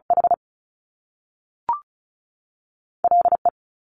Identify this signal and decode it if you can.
Morse code, heard as a keyed tone, about 36 wpm, 700 Hz. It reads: H L E
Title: Top Two Letter Combinations in Words 35wpm